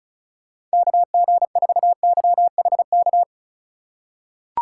Morse code, heard as a keyed tone, about 36 wpm, 700 Hz. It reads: KG4YHK
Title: USA Call Signs - 6 Characters - 4x Character Spacing - 35wpm